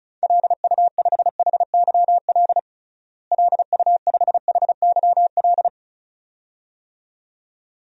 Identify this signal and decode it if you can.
Morse code, heard as a keyed tone, about 34 wpm, 700 Hz. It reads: LU5HYL LU5HYL
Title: DXCC-100 Call Signs - 6 Characters - Repeated 2x - 35wpm